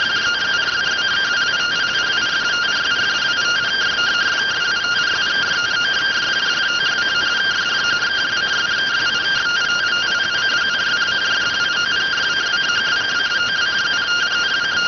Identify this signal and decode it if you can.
Two-tone teleprinter signal, about 170 Hz shift.